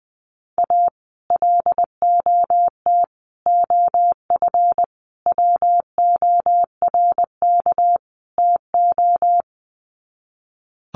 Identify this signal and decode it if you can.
Morse code, heard as a keyed tone, about 20 wpm, 700 Hz. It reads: A LOT OF WORK TO
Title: Sets of 5 Words Encore2 20wpm